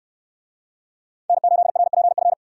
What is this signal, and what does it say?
Morse code, heard as a keyed tone, about 66 wpm, 700 Hz. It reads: N8RPF